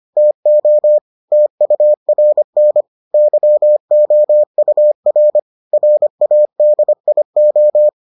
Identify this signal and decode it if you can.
Morse code, heard as a keyed tone, about 25 wpm, 600 Hz. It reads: TO TURN YOUR RADIO